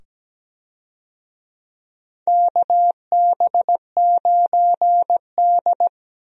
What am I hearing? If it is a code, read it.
Morse code, heard as a keyed tone, about 17 wpm, 700 Hz. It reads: KB9D
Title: USA Call Signs - 4 Characters - 17wpm